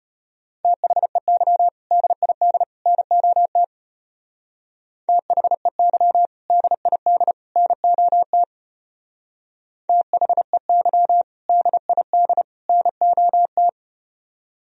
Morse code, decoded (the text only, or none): THEY DID NOT THEY DID NOT THEY DID NOT